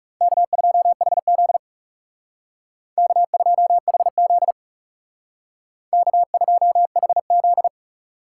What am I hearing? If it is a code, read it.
Morse code, heard as a keyed tone, about 36 wpm, 700 Hz. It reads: K2HZ K2HZ K2HZ
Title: Speed Racing - USA 1x2 Call Signs 30wpm